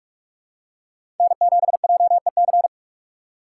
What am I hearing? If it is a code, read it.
Morse code, heard as a keyed tone, about 42 wpm, 700 Hz. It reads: N 7 J E C
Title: USA Call Signs - 5 Characters - 6x Character Spacing - 45wpm